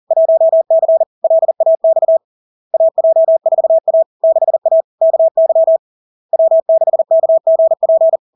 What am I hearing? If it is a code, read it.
Morse code, heard as a keyed tone, about 40 wpm, 650 Hz. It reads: 1C LAX AJ4A 6A KY W6KGP
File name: LARK-FD-2022-Log-in-CW-40wpm-650-Hz